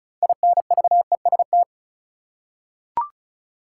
Morse code, heard as a keyed tone, about 34 wpm, 700 Hz. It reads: INVEST
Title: Six Letter Words - 35wpm